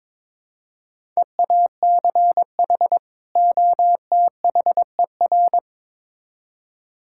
Morse code, heard as a keyed tone, about 22 wpm, 700 Hz. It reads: EACH OTHER
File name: Build A Phrase - 2-Word Phrases - 22wpm